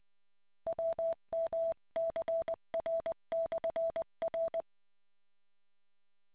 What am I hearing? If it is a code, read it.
Morse code, heard as a keyed tone, about 20 wpm, 660 Hz. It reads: WMCR/R